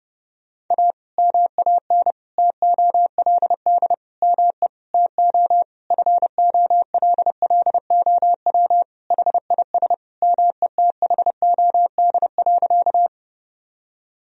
Morse code, heard as a keyed tone, about 29 wpm, 700 Hz. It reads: A MAN TOLD ME TO FOLLOW HIS METHOD.
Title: Sentences from Top 1000 Words - Review Part 3 - 30wpm